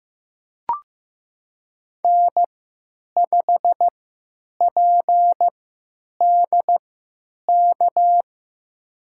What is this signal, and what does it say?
Morse code, heard as a keyed tone, about 15 wpm, spaced as for 9 wpm, 700 Hz. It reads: N5PDK
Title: USA Call Signs - 3x Character Spacing - 15wpm